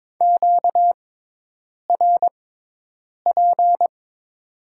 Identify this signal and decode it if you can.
Morse code, heard as a keyed tone, about 21 wpm, spaced as for 8 wpm, 700 Hz. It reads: QRP
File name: Top 90 QSO Elements - 6x Character Spacing - 22wpm